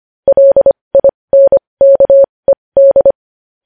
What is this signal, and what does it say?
Morse code, heard as a keyed tone, about 25 wpm, spaced as for 20 wpm, 550 Hz. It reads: LINKED
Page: News Headlines in Morse Code at 20 WPM